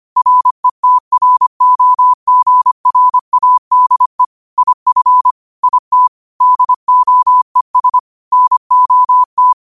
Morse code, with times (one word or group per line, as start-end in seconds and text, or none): 0.2-4.2 RETROGRADE
4.6-5.3 IF
5.6-6.1 IT
6.4-8.0 DOES
8.3-9.5 NOT